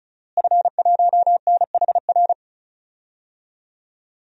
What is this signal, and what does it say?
Morse code, heard as a keyed tone, about 36 wpm, 700 Hz. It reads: F1NHR
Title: DXCC-100 Call Signs - 5 Characters - Repeated 2x - 35wpm